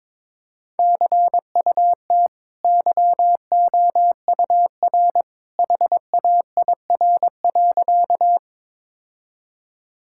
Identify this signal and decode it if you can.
Morse code, heard as a keyed tone, about 21 wpm, 690 Hz. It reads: CUT YOUR HAIR.
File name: practice / 22wpm / Easy Sentences with Hints - 22wpm